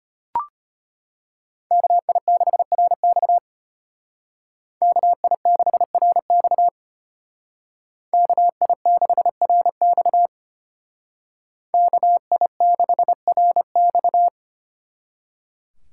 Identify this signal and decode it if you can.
Morse code, heard as a keyed tone, about 33 wpm, 700 Hz. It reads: KI6RX KI6RX KI6RX KI6RX